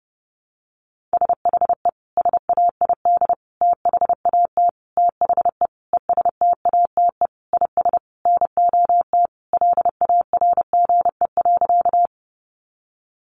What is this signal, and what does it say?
Morse code, heard as a keyed tone, about 30 wpm, 700 Hz. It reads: SHE SAID THAT THE ESTATE IS NOT LARGE.
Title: Sentences from Top 1200 Words 30wpm